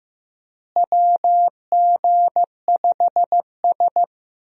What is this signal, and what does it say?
Morse code, heard as a keyed tone, about 16 wpm, 700 Hz. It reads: WG5S
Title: USA Call Signs - 4 Characters - Repeated 2x - 15wpm